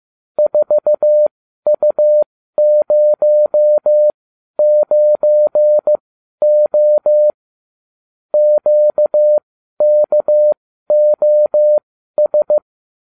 Morse code, heard as a keyed tone, about 15 wpm, spaced as for 12 wpm, 600 Hz. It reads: 4U09O QKOS